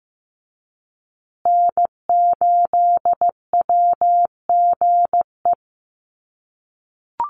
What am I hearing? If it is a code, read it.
Morse code, heard as a keyed tone, about 15 wpm, 700 Hz. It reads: N8WGE
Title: USA Call Signs Repeated 3x 15wpm